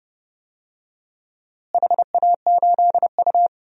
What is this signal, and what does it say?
Morse code, heard as a keyed tone, about 30 wpm, 700 Hz. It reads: HA8U
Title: DXCC-100 Call Signs - 4 Characters - Repeated 2x - 30wpm